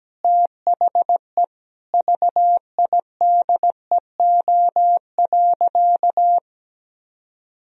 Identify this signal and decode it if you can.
Morse code, heard as a keyed tone, about 17 wpm, 700 Hz. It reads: THE VIDEO.
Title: Sentences from Top 1000 Words - Review Part 4 - 17wpm